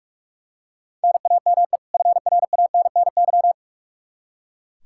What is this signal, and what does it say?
Morse code, heard as a keyed tone, about 44 wpm, 700 Hz. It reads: NAME FRANNY
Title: Practice QSO - Exchange 1 - Beginning - 45wpm